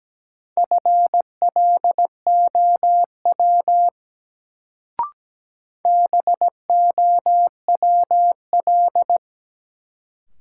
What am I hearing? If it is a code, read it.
Morse code, heard as a keyed tone, about 17 wpm, 700 Hz. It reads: FLOW BOWL
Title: Sentences from Top 1200 Words - New Word Drill - 17wpm